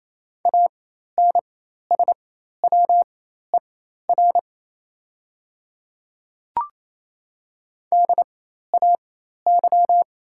Morse code, heard as a keyed tone, about 28 wpm, spaced as for 13 wpm, 700 Hz. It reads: ANSWER DAY
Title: 4x Character Spacing - Top 300 Words - 28wpm